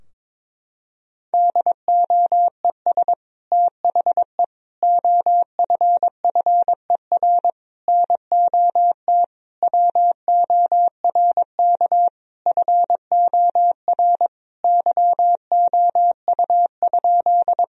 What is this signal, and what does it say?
Morse code, heard as a keyed tone, about 21 wpm, 700 Hz. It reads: DOES THE OFFER NOT WORK FOR YOU?
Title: Sentences from Top 1000 Words - Review Part 4 - 22wpm